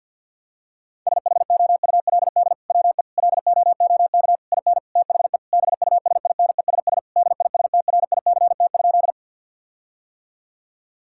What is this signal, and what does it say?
Morse code, heard as a keyed tone, about 50 wpm, 700 Hz. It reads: SHOULD WE LOOK IN THE BUSINESS DISTRICT?
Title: Sentences from Top 1600 Words 50wpm